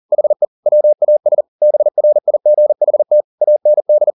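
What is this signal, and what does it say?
Morse code, heard as a keyed tone, about 40 wpm, 600 Hz. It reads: HE WAS BRIGHT AND
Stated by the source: Morse code at 40 wpm, practice text, CW med Gnister.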